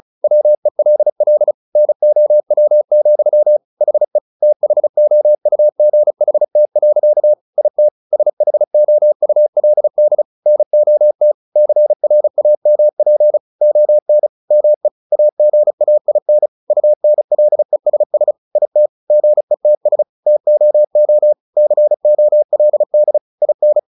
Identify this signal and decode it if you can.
Morse code, heard as a keyed tone, about 34 wpm, 600 Hz. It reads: WELL NOW, HE THOUGHT. IT SHOULD NOT CRAMP ON ME AGAIN UNLESS IT GETS TOO COLD IN
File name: the old man and the sea 2 35wpm